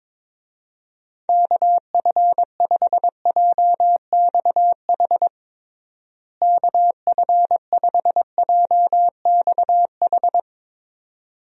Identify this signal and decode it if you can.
Morse code, heard as a keyed tone, about 22 wpm, 700 Hz. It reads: KF5JXH KF5JXH
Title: USA Call Signs - 6 Characters - Repeated 2x - 22wpm